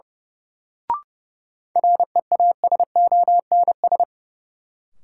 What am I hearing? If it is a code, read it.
Morse code, heard as a keyed tone, about 30 wpm, 700 Hz. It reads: REASONS
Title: Sets of 4 Words - New Word Drill - 30wpm